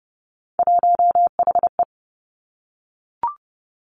Morse code, heard as a keyed tone, about 30 wpm, 700 Hz. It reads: 1HE
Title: Three Challenging Letter-Number Combinations 30wpm